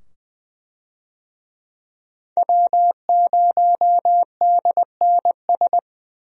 Morse code, heard as a keyed tone, about 20 wpm, 700 Hz. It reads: W0DNS
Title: USA Call Signs - 5 Characters - 20wpm